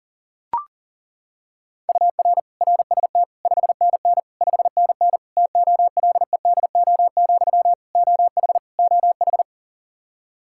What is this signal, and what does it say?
Morse code, heard as a keyed tone, about 40 wpm, 700 Hz. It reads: UR RST 5NN 5NN TOLEDO, OH OH
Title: Practice QSO - Exchange 1 - Beginning - 40wpm